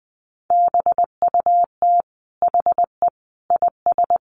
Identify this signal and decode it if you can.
Morse code, heard as a keyed tone, about 20 wpm, 700 Hz. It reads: BUT HE IS
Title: Sentences from Top 1400 Words 20wpm